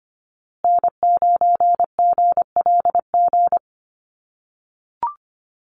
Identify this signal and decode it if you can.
Morse code, heard as a keyed tone, about 25 wpm, 700 Hz. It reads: N9GLG
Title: USA Call Signs Repeated 2x 25wpm